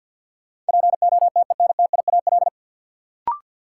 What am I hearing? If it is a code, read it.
Morse code, heard as a keyed tone, about 50 wpm, 700 Hz. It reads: POTENTIAL
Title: Sentences from Top 1000 Words - Review Part 4 - New Word Drill - 50wpm